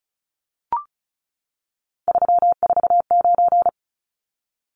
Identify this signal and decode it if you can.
Morse code, heard as a keyed tone, about 34 wpm, 700 Hz. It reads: 349
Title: Numbers - 3-Digit 35wpm